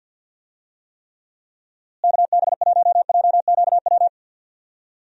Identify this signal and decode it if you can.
Morse code, heard as a keyed tone, about 51 wpm, 700 Hz. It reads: KB1JQW
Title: USA Call Signs - 6 Characters - Repeated 3x - 50wpm